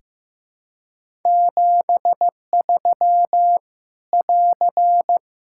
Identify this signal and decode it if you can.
Morse code, heard as a keyed tone, about 15 wpm, 700 Hz. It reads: 73 <AR>
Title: K1USN SST QSO Practice 15wpm